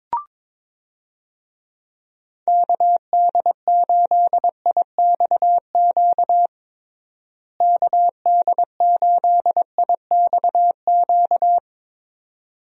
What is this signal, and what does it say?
Morse code, heard as a keyed tone, about 22 wpm, 700 Hz. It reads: KD8IXQ KD8IXQ
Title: USA Call Signs - 6 Characters - Repeated 2x - 22wpm